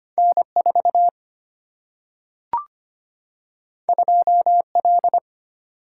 Morse code, heard as a keyed tone, about 25 wpm, 700 Hz. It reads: N4 2L
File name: Two Letter-Number Combinations - 25wpm